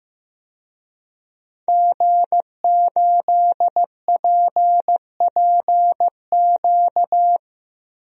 Morse code, heard as a keyed tone, about 15 wpm, 700 Hz. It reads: G8PPQ
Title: DXCC-100 Call Signs - 5 Characters - Repeated 3x - 15wpm